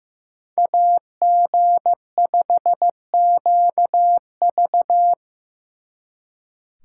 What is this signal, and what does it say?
Morse code, heard as a keyed tone, about 17 wpm, 700 Hz. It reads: AG5QV
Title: Speed Racing - USA Call Signs 15wpm